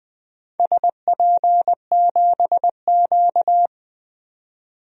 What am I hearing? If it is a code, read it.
Morse code, heard as a keyed tone, about 20 wpm, 700 Hz. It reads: SP7Q
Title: DXCC-100 Call Signs - 4 Characters - 20wpm